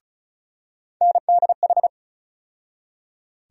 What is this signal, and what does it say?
Morse code, heard as a keyed tone, about 36 wpm, 700 Hz. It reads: NDH
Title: USA Call Sign Postfixes - 3 Letters 35wpm